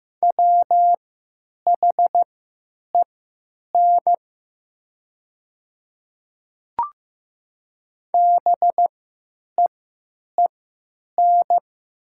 Morse code, heard as a keyed tone, about 15 wpm, spaced as for 9 wpm, 700 Hz. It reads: WHEN BEEN
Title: 3x Character Spacing - Top 100 Words - 15wpm